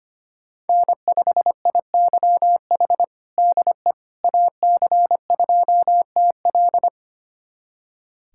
Morse code, heard as a keyed tone, about 25 wpm, 700 Hz. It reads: N5IYH DE AC2TL
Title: USA Callsign DE Callsign - 5 Characters - 25wpm